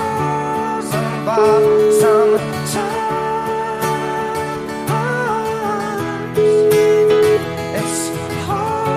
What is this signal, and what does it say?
Ringback tone (Europe, 425 Hz), with noise.